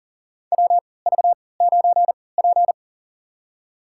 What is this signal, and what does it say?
Morse code, heard as a keyed tone, about 40 wpm, spaced as for 23 wpm, 700 Hz. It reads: WV9P
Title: USA Call Signs - 4 Characters - 3x Character Spacing - 40wpm